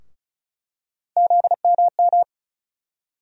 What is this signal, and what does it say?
Morse code, heard as a keyed tone, about 36 wpm, 700 Hz. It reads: ZMM
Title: USA Call Sign Postfixes - 3 Letters 35wpm